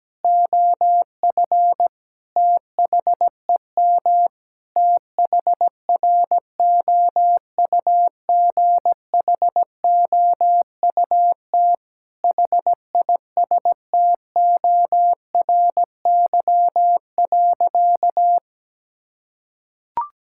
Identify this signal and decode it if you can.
Morse code, heard as a keyed tone, about 17 wpm, 700 Hz. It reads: OF THEM THROUGHOUT HISTORY.
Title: Sentences from Top 1000 Words - Review Part 5 - 17wpm